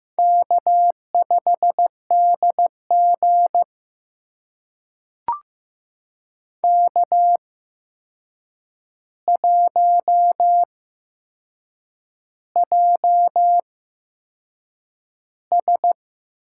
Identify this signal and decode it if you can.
Morse code, heard as a keyed tone, about 15 wpm, spaced as for 6 wpm, 700 Hz. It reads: K5DG K1JS
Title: USA Call Signs - 8x Character Spacing - 15wpm